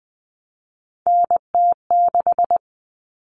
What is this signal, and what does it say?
Morse code, heard as a keyed tone, about 20 wpm, 700 Hz. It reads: NT6
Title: All Cut Numbers - 3-Digit 20wpm